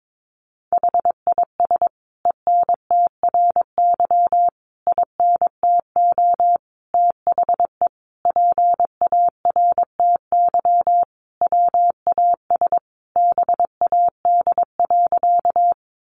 Morse code, heard as a keyed tone, about 22 wpm, 700 Hz. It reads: HIS ENTRY INTO THE PARTY WAS BAD.
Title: Sentences from Top 1500 Words 22wpm